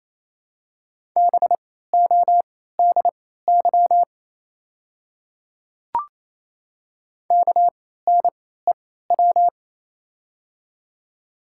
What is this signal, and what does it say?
Morse code, heard as a keyed tone, about 28 wpm, spaced as for 16 wpm, 700 Hz. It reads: BODY KNEW
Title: 3x Character Spacing - Top 500 Words - 28wpm